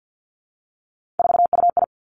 Morse code, heard as a keyed tone, about 50 wpm, 700 Hz. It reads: 4UI
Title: Three Challenging Letter-Number Combinations 50wpm